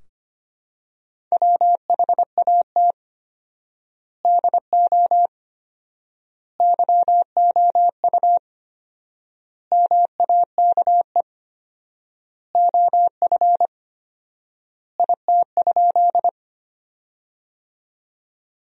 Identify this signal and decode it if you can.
Morse code, heard as a keyed tone, about 25 wpm, 700 Hz. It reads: WHAT DO YOU MAKE OF IT?
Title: 4x Word-Spacing - Sentences from Top 100 Words - 25wpm